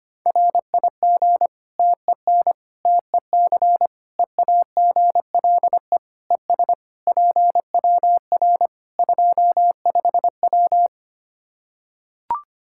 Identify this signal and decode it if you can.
Morse code, heard as a keyed tone, about 25 wpm, 700 Hz. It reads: RIG TEN TEC EAGLE ES PWR 25W
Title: K1USN SST QSO Practice 25wpm